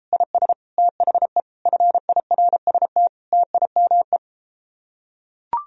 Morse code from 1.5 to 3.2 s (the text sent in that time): FIRST